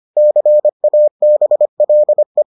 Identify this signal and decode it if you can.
Morse code, heard as a keyed tone, about 25 wpm, 600 Hz. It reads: CABLE